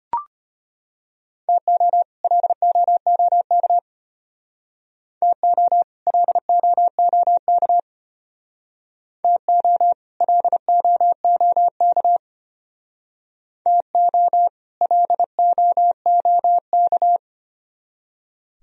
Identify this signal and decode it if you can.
Morse code, heard as a keyed tone, about 33 wpm, 700 Hz. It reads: TO LOOK TO LOOK TO LOOK TO LOOK